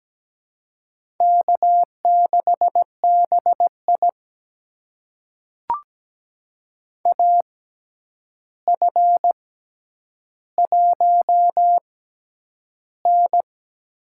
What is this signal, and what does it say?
Morse code, heard as a keyed tone, about 17 wpm, spaced as for 9 wpm, 700 Hz. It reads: K6BI AF1N
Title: USA Call Signs - 4 Characters - 6x Character Spacing - 17wpm